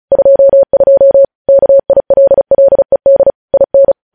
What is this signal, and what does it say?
Morse code, heard as a keyed tone, about 36 wpm, 550 Hz. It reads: 22 KILLED IN